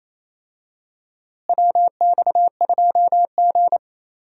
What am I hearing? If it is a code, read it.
Morse code, heard as a keyed tone, about 28 wpm, 700 Hz. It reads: WX2G
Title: USA Call Signs - 4 Characters - Repeated 2x - 28wpm